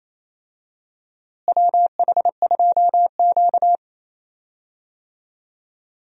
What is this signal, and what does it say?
Morse code, heard as a keyed tone, about 28 wpm, 700 Hz. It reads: WH2Q
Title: USA Call Signs - 4 Characters - Repeated 3x - 28wpm